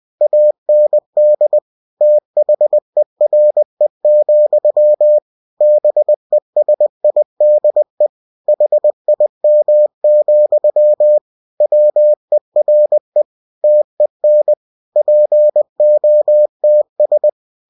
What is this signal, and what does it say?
Morse code, heard as a keyed tone, about 20 wpm, 600 Hz. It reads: AND THERE, BESIDE HIM, WERE TEN POTS
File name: cw-books / winnie-the-pooh / winnie-20wpm-page-0094